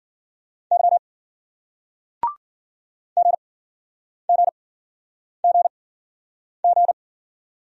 Morse code, heard as a keyed tone, about 50 wpm, spaced as for 9 wpm, 700 Hz. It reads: Y GGGG
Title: Speed-Racing Single Letters 40wpm